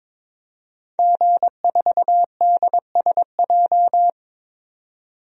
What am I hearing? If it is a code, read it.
Morse code, heard as a keyed tone, about 22 wpm, 700 Hz. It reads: G4DSJ
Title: DXCC-100 Call Signs - 5 Characters - 22wpm